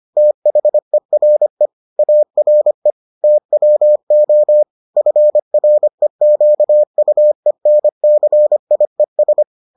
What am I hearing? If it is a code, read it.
Morse code, heard as a keyed tone, about 25 wpm, 600 Hz. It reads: THERE ARE TWO FREQUENCIES